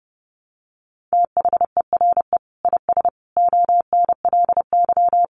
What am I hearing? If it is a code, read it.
Morse code, heard as a keyed tone, about 30 wpm, 700 Hz. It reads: THERE IS ONLY
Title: Sentences from Top 300 Words 30wpm